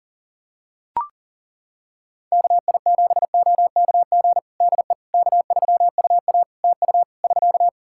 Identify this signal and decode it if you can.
Morse code, heard as a keyed tone, about 40 wpm, 700 Hz. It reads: KI7OKG DE K3UA TU <SK>
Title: Practice QSO - Exchange 4 - Ending - 40wpm